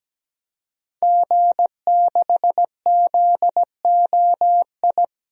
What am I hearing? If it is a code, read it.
Morse code, heard as a keyed tone, about 17 wpm, 700 Hz. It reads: G6ZOI
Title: DXCC-100 Call Signs - 5 Characters - 17wpm